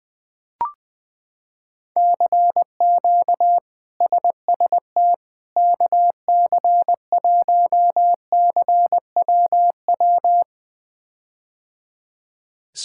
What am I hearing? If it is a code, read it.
Morse code, heard as a keyed tone, about 20 wpm, 700 Hz. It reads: CQ SST KC1CWW
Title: K1USN SST QSO Practice - 20wpm